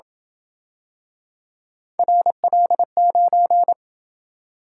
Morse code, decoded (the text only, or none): RL9